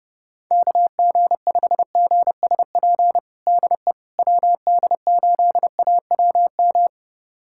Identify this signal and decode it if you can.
Morse code, heard as a keyed tone, about 30 wpm, 700 Hz. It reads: KG5GSP DE WD8AWM
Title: USA Callsign DE Callsign 30wpm